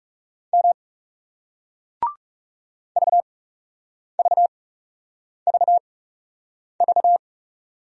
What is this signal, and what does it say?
Morse code, heard as a keyed tone, about 40 wpm, spaced as for 8 wpm, 700 Hz. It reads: M VVVV